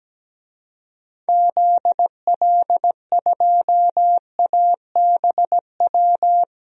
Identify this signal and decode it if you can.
Morse code, heard as a keyed tone, about 17 wpm, 700 Hz. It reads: ZL2ABW
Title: DXCC-100 Call Signs - 6 Characters - 17wpm